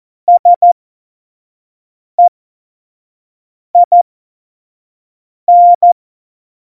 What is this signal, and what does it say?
Morse code, heard as a keyed tone, about 14 wpm, spaced as for 5 wpm, 700 Hz. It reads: SEIN
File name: Morsekurs / Witze / Witz 07 Juli 2020, 25 F70